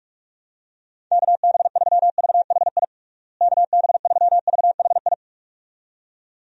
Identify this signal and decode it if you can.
Morse code, heard as a keyed tone, about 46 wpm, 700 Hz. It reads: KB3VHI KB3VHI
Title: USA Call Signs - 6 Characters - Repeated 2x - 45wpm